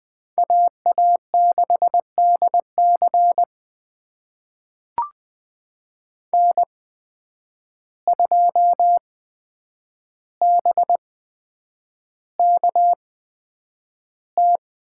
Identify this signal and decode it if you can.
Morse code, heard as a keyed tone, about 20 wpm, spaced as for 9 wpm, 700 Hz. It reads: AA6DC N2BKT
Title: USA Call Signs - 5 Characters - 8x Character Spacing - 20wpm